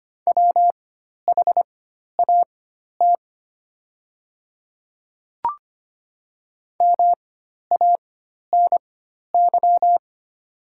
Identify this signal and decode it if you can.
Morse code, heard as a keyed tone, about 25 wpm, spaced as for 12 wpm, 700 Hz. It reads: WHAT MANY
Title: 4x Character Spacing - Top 100 Words - 25wpm